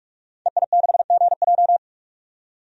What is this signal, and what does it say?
Morse code, heard as a keyed tone, about 47 wpm, 700 Hz. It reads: EI6GJ
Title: DXCC-100 Call Signs - 5 Characters - 45wpm